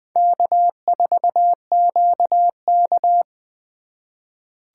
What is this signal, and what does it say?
Morse code, heard as a keyed tone, about 20 wpm, 700 Hz. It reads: K4QK
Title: USA Call Signs - 3x Character Spacing - 20wpm